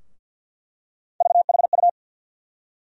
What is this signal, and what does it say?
Morse code, heard as a keyed tone, about 48 wpm, 700 Hz. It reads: VHU
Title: USA Call Sign Postfixes - 3 Letters 50wpm